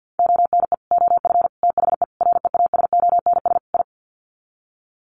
Morse code, heard as a keyed tone, about 50 wpm, 700 Hz. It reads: ONE OF THE REASONS I
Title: Sets of 5 Words Encore2 50wpm